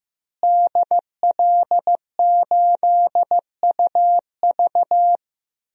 Morse code, heard as a keyed tone, about 15 wpm, 700 Hz. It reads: DL8UV
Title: DXCC-100 Call Signs - 5 Characters - Repeated 2x - 15wpm